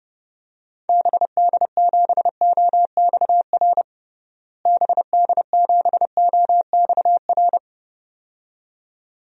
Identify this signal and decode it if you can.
Morse code, heard as a keyed tone, about 32 wpm, 700 Hz. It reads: BD7OXR BD7OXR
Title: DXCC-100 Call Signs - 6 Characters - Repeated 2x - 30wpm